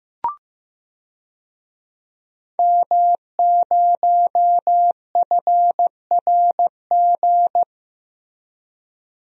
Morse code, heard as a keyed tone, about 15 wpm, 700 Hz. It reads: M0FRG
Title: DXCC-100 Call Signs - 5 Characters - Repeated 2x - 15wpm